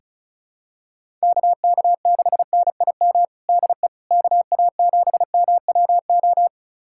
Morse code, heard as a keyed tone, about 35 wpm, 700 Hz. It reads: KK6NIM DE KA7MWO
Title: USA Callsign DE Callsign - 6 Characters - 35wpm